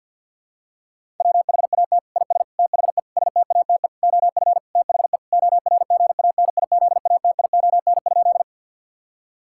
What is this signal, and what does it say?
Morse code, heard as a keyed tone, about 50 wpm, 700 Hz. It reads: WHAT IS THE STATE OF THE ORGANIZATION?
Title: Sentences from Top 1000 Words - Review Part 5 - 50wpm